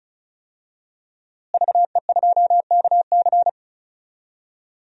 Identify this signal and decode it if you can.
Morse code, heard as a keyed tone, about 34 wpm, 700 Hz. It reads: VE2KC
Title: DXCC-100 Call Signs - 5 Characters - Repeated 3x - 35wpm